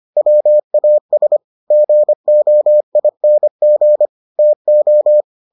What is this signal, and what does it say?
Morse code, heard as a keyed tone, about 25 wpm, 600 Hz. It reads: WAS GOING TO